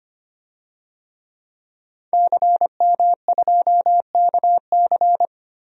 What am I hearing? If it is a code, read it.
Morse code, heard as a keyed tone, about 25 wpm, 700 Hz. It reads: CM2KC
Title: DXCC-100 Call Signs - 5 Characters - 25wpm